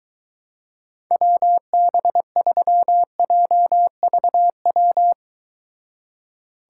Morse code, heard as a keyed tone, about 23 wpm, 700 Hz. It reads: WB3JVW